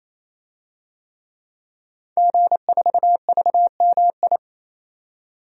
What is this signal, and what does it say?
Morse code, heard as a keyed tone, about 28 wpm, 700 Hz. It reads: G4VMI
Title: DXCC-100 Call Signs - 5 Characters - 28wpm